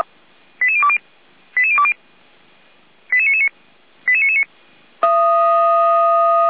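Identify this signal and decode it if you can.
Five-tone selective call (ZVEI), heard as quick runs of five tones, tones of about 80 ms. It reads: 88020 88020 88999 88999
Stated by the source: Received by SDR; ZVEI five-tone code 88125.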